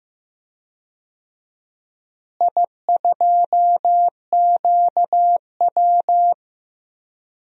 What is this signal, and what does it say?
Morse code, heard as a keyed tone, about 15 wpm, 700 Hz. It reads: I2QW
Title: DXCC-100 Call Signs - 4 Characters - Repeated 3x - 15wpm